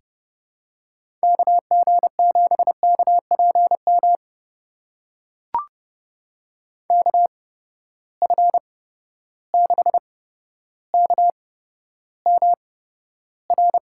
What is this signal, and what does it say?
Morse code, heard as a keyed tone, about 30 wpm, spaced as for 13 wpm, 700 Hz. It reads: KG7KPM KF6KMR
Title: USA Call Signs - 6 Characters - 8x Character Spacing - 30wpm